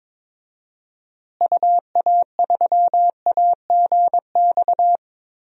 Morse code, heard as a keyed tone, about 22 wpm, 700 Hz. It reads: UA3AGX
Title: DXCC-100 Call Signs - 6 Characters - Repeated 3x - 22wpm